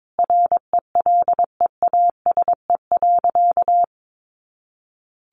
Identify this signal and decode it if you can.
Morse code, heard as a keyed tone, about 22 wpm, 700 Hz. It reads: RELEASE.
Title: Sentences from Top 1100 Words 22wpm